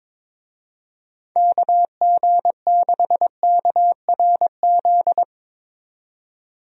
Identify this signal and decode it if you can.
Morse code, heard as a keyed tone, about 22 wpm, 700 Hz. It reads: KG6KRZ
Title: USA Call Signs - 6 Characters - 22wpm